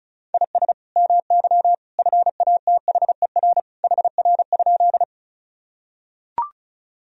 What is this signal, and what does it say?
Morse code, heard as a keyed tone, about 36 wpm, 700 Hz. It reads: IS MY FATHER HR?
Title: Sentences from Top 300 Words and Abbreviations 35wpm